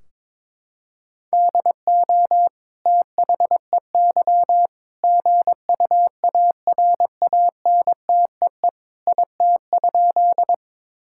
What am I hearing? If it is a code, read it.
Morse code, heard as a keyed tone, about 22 wpm, 700 Hz. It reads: DO THEY GUARANTEE IT?
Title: Sentences from Top 1600 Words 22wpm